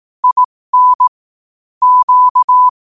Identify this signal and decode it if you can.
Morse code, heard as a keyed tone, about 18 wpm, spaced as for 15 wpm, 1000 Hz. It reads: IN Q
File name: qotdfiles / 15wpm